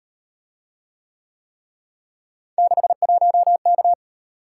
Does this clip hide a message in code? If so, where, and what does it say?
Yes: Morse code from 2.6 to 3.9 s, heard as a keyed tone, about 37 wpm, 700 Hz. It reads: 61K